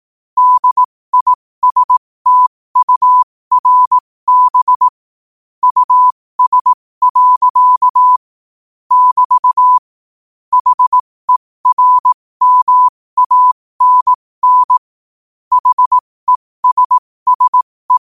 Morse code, heard as a keyed tone, about 18 wpm, spaced as for 15 wpm, 1000 Hz. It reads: DISTURB US. <BT> HERMANN HESSE
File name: qotdfiles / 15wpm